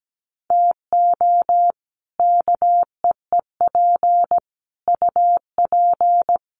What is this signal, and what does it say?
Morse code, heard as a keyed tone, about 17 wpm, 700 Hz. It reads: TO KEEP UP
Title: Sets of 5 Words Encore 17wpm